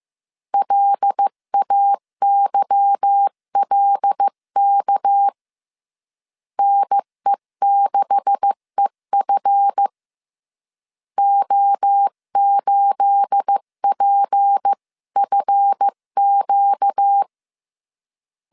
Morse code, heard as a keyed tone, about 15 wpm, 800 Hz. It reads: LAYLK NE6EF O8PFQ
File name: CW / DET / 1 9 1500C S B 14wpm 3302